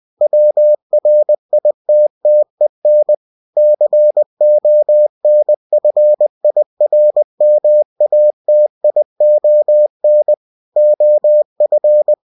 Morse code, heard as a keyed tone, about 20 wpm, 600 Hz. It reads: WRITTEN CONFIRMATION OF